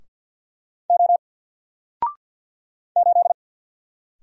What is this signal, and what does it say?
Morse code, heard as a keyed tone, about 48 wpm, 700 Hz. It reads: O 8
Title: Single Character (Letter:Number:Punctuation:Prosign) Alt 50wpm